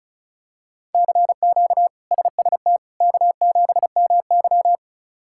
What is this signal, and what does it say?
Morse code, heard as a keyed tone, about 34 wpm, 700 Hz. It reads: CQ SST K7MY
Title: K1USN SST QSO Practice - 35wpm